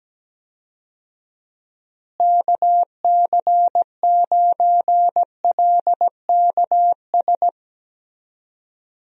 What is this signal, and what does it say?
Morse code, heard as a keyed tone, about 17 wpm, 700 Hz. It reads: KC9LKS
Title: USA Call Signs - 6 Characters - Repeated 3x - 17wpm